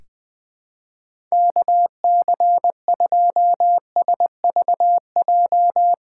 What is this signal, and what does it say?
Morse code, heard as a keyed tone, about 20 wpm, 700 Hz. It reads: KC2SVJ
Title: USA Callsign DE Callsign - 6 Characters - 20wpm